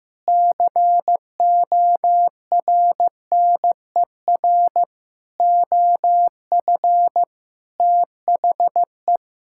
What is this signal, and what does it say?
Morse code, heard as a keyed tone, about 15 wpm, 700 Hz. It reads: CORNER OF THE
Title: Sentences from Top 1600 Words 15wpm